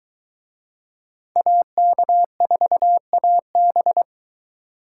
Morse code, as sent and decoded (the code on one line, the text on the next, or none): .- -.- ....- .- -...
AK4AB